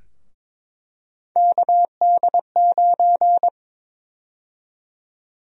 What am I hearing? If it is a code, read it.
Morse code, heard as a keyed tone, about 22 wpm, 700 Hz. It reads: KD9
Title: USA Call Sign Prefixes - 22wpm